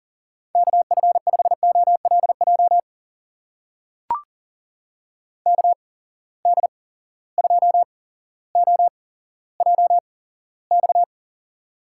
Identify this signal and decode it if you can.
Morse code, heard as a keyed tone, about 40 wpm, spaced as for 17 wpm, 700 Hz. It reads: KF5OLJ KD2OJX
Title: USA Call Signs - 6 Characters - 8x Character Spacing - 40wpm